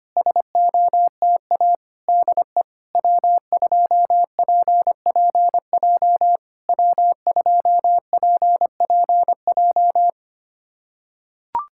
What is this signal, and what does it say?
Morse code, heard as a keyed tone, about 25 wpm, 700 Hz. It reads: SOTA DE W2PPJ W2PPJ
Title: SOTA QSO Practice 25wpm